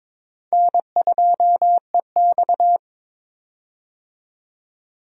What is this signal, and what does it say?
Morse code, heard as a keyed tone, about 21 wpm, 700 Hz. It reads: N2EX
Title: USA Call Signs - 4 Characters - 22wpm